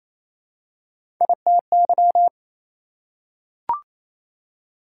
Morse code, heard as a keyed tone, about 27 wpm, 700 Hz. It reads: ITY
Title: Prefixes and Suffixes 28wpm